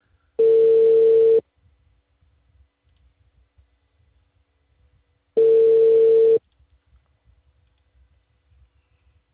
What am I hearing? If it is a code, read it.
Ringback tone (China, 450 Hz).